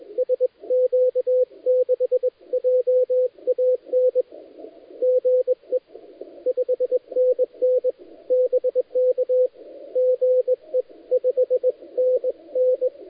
Morse code, heard as a keyed tone, about 21 wpm, 510 Hz, with radio noise. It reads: SQ6JAN GE 5NN BK GE 5NN
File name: dlazaawansowanych / nagraniazpasma / titawaNETcontrol